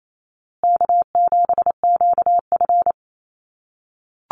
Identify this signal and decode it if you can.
Morse code, heard as a keyed tone, about 28 wpm, 700 Hz. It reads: K7QF
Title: USA Call Signs Repeated 2x 28wpm